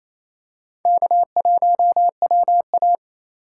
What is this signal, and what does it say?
Morse code, heard as a keyed tone, about 28 wpm, 700 Hz. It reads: K1WA
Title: USA Call Signs - 3x Character Spacing - 28wpm